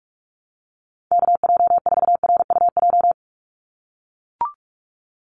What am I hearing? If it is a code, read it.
Morse code, heard as a keyed tone, about 44 wpm, 700 Hz. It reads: KJ4RUJ
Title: USA Call Signs 45wpm